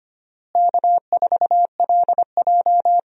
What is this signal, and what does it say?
Morse code, heard as a keyed tone, about 25 wpm, 700 Hz. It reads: K4LJ
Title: USA Call Signs - 4 Characters - Repeated 3x - 25wpm